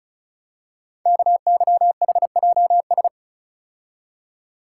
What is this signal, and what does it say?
Morse code, heard as a keyed tone, about 36 wpm, 700 Hz. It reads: KYHJS
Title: Code Groups - 5 Letters - 35wpm